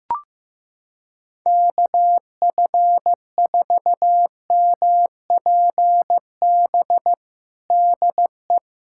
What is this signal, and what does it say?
Morse code, heard as a keyed tone, about 15 wpm, 700 Hz. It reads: KF4MPB DE
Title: USA Callsign DE Callsign 15wpm